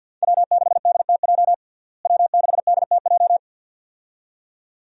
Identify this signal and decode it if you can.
Morse code, heard as a keyed tone, about 48 wpm, 700 Hz. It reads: W6DTJ W6DTJ
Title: USA Call Signs - 5 Characters - Repeated 2x - 50wpm